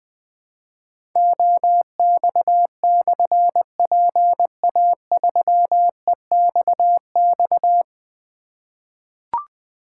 Morse code, heard as a keyed tone, about 21 wpm, 700 Hz. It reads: OX/PA3EXX
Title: Difficult Call Signs 20wpm